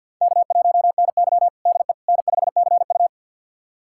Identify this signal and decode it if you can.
Morse code, heard as a keyed tone, about 50 wpm, 700 Hz. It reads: K1NY DE N5CU
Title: USA Callsign DE Callsign - 4 Characters - 50wpm